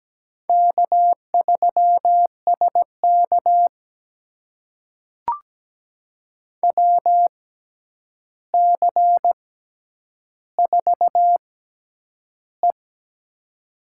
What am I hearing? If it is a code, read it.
Morse code, heard as a keyed tone, about 17 wpm, spaced as for 9 wpm, 700 Hz. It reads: K3SK WC4E
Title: USA Call Signs - 4 Characters - 6x Character Spacing - 17wpm